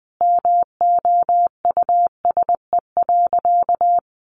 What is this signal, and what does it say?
Morse code, heard as a keyed tone, about 20 wpm, 700 Hz. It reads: MOUSE.
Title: A Sound Of Thunder 20wpm